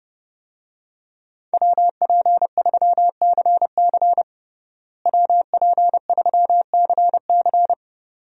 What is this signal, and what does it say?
Morse code, heard as a keyed tone, about 30 wpm, 700 Hz. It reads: WP3CC WP3CC
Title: USA Call Signs - 5 Characters - Repeated 2x - 30wpm